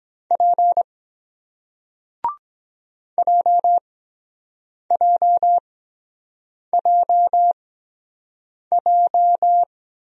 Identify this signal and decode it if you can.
Morse code, heard as a keyed tone, about 23 wpm, spaced as for 7 wpm, 700 Hz. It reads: P JJJJ